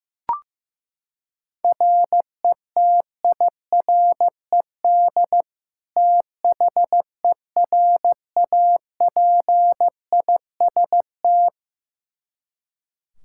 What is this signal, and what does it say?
Morse code, heard as a keyed tone, about 16 wpm, 700 Hz. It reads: RETIRED THERAPIST
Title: Practice QSO - Exchange 3 - More Info - 15wpm